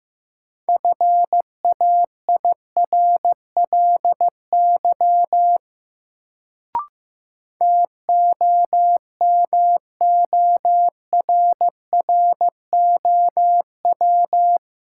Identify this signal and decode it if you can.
Morse code, heard as a keyed tone, about 15 wpm, 700 Hz. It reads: FAIRLY TOMORROW
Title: Sentences from Top 1100 Words - New Word Drill - 15wpm